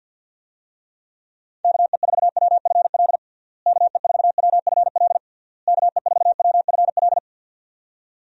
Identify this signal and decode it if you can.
Morse code, heard as a keyed tone, about 46 wpm, 700 Hz. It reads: KE4WFL KE4WFL KE4WFL
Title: USA Call Signs - 6 Characters - Repeated 3x - 50wpm